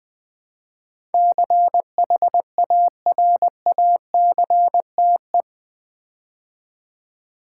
Morse code, heard as a keyed tone, about 20 wpm, 700 Hz. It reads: CHARACTE
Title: Build A Word - 4 Letters At A Time - 12-Letter Words - 20wpm